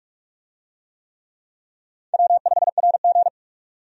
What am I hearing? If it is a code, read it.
Morse code, heard as a keyed tone, about 43 wpm, 700 Hz. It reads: W5RG
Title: USA Call Signs - 4 Characters - 45wpm